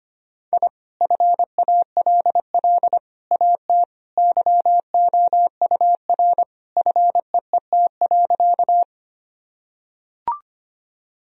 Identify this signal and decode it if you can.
Morse code, heard as a keyed tone, about 25 wpm, 700 Hz. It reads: I FALL AT YOUR FEET.
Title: Sentences from Top 1000 Words - Review Part 2 - 25wpm